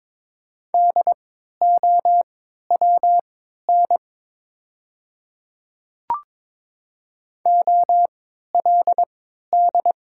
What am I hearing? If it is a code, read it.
Morse code, heard as a keyed tone, about 22 wpm, spaced as for 12 wpm, 700 Hz. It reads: DOWN OLD
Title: 3x Character Spacing - Top 300 Words - 22wpm